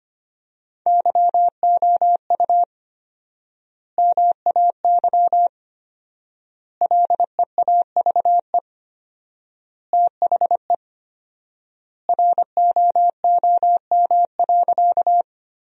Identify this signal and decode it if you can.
Morse code, heard as a keyed tone, about 25 wpm, 700 Hz. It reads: YOU MAY LEAVE THE ROOM.
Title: 4x Word-Spacing - Sentences from Top 400 Words - 25wpm